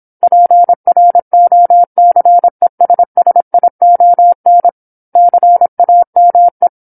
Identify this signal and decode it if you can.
Morse code, heard as a keyed tone, about 27 wpm, 700 Hz. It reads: PROCESSION CAME
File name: 08-Chapter-08 26 WPM